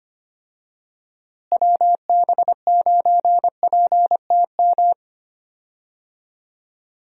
Morse code, heard as a keyed tone, about 25 wpm, 700 Hz. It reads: WB9PTM
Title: USA Call Signs - 6 Characters - Repeated 3x - 25wpm